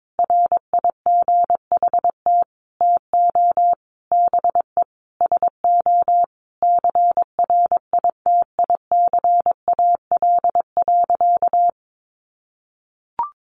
Morse code, heard as a keyed tone, about 22 wpm, 700 Hz. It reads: RIGHT TO BE SO CRITICAL.
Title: Sentences from Top 1100 Words 22wpm